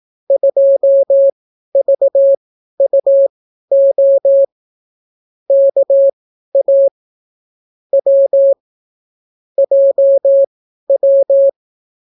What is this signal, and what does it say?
Morse code, heard as a keyed tone, about 18 wpm, spaced as for 12 wpm, 550 Hz. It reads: 2VUO KA W JW